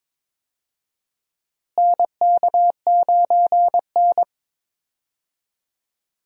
Morse code, heard as a keyed tone, about 21 wpm, 700 Hz. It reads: NK9N
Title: USA Call Signs - 4 Characters - Repeated 3x - 22wpm